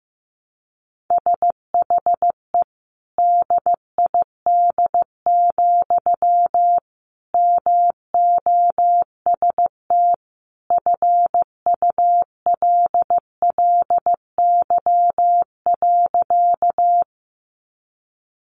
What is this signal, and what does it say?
Morse code, heard as a keyed tone, about 15 wpm, 700 Hz. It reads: SHE DID, MOST FULLY.
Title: Sentences from Top 1300 Words 15wpm